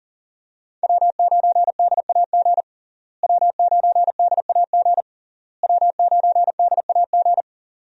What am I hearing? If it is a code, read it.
Morse code, heard as a keyed tone, about 40 wpm, 700 Hz. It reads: W9DAG W9DAG W9DAG
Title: USA Call Signs - 5 Characters - Repeated 3x - 40wpm